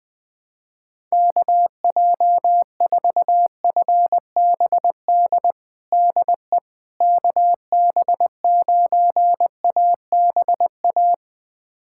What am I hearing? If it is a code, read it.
Morse code, heard as a keyed tone, about 20 wpm, 700 Hz. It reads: KJ4FBD DE KB9ABA
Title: USA Callsign DE Callsign - 6 Characters - 20wpm